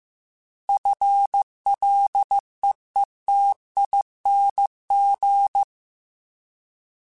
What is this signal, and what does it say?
Morse code, heard as a keyed tone, about 15 wpm, 800 Hz. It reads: FLEETING
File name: FL / FL mp3s / FL Prefixes 3x2 sp @13